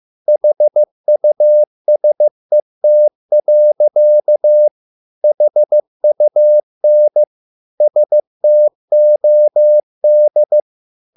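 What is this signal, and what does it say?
Morse code, heard as a keyed tone, about 15 wpm, 600 Hz. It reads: HUSET. HUN STOD
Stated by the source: Morse code at a slow 15 wpm from CW med Gnister.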